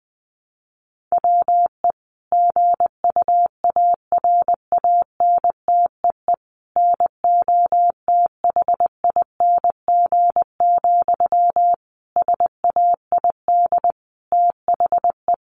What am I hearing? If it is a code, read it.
Morse code, heard as a keyed tone, about 19 wpm, 700 Hz. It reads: WE GUARANTEE NOTHING, SAID THE
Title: A Sound Of Thunder 20wpm